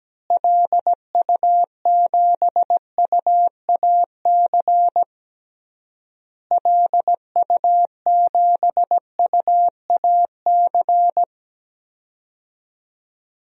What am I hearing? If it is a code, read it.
Morse code, heard as a keyed tone, about 17 wpm, 700 Hz. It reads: LU7UAC LU7UAC
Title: DXCC-100 Call Signs - 6 Characters - Repeated 3x - 17wpm